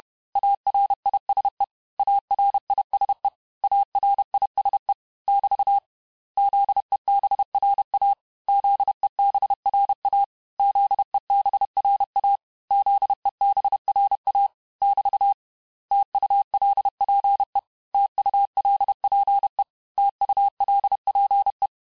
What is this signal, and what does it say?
Morse code, heard as a keyed tone, about 31 wpm, 800 Hz. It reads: ARISE ARISE ARISE <BT> ZEBRA ZEBRA ZEBRA ZEBRA <BT> TULPE TULPE TULPE
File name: cours-du-mercredi / audios / cours-861-texte-les-mots-vitesse-29MPM